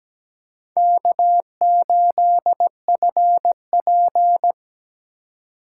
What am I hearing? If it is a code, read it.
Morse code, heard as a keyed tone, about 17 wpm, 700 Hz. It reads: K8FP
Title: USA Call Signs - 2x Character Spacing - 17wpm